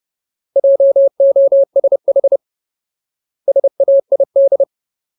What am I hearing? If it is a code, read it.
Morse code, heard as a keyed tone, about 30 wpm, 550 Hz. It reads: JOSH SAID
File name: Ultimate Word List / 30WPM Ultimate